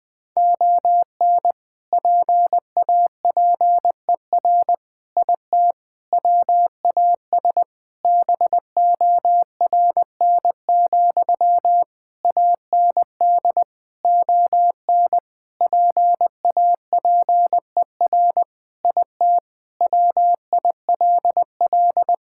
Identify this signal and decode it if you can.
Morse code, heard as a keyed tone, about 20 wpm, 700 Hz. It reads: ON PAPER IT WAS BORN, AND ON PAPER IT WILL
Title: Sentences from Top 1600 Words 20wpm